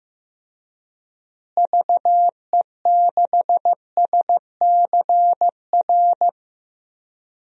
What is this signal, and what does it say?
Morse code, heard as a keyed tone, about 15 wpm, 700 Hz. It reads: VE6SCR
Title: DXCC-100 Call Signs - 6 Characters - Repeated 3x - 15wpm